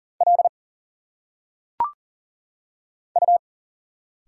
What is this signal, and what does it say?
Morse code, heard as a keyed tone, about 40 wpm, 700 Hz. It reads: L U
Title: Single Letters - Phonetic Pronunciation 40wpm